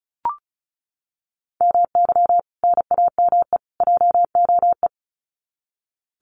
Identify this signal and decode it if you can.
Morse code, heard as a keyed tone, about 34 wpm, 700 Hz. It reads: MY NAME JOE
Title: Common QSO Phrases 35wpm